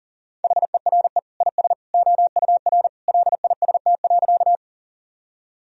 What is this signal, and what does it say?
Morse code, heard as a keyed tone, about 40 wpm, 700 Hz. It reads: HERE IS OUR LIST.
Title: Sentences from Top 1000 Words - Review Part 2 - 40wpm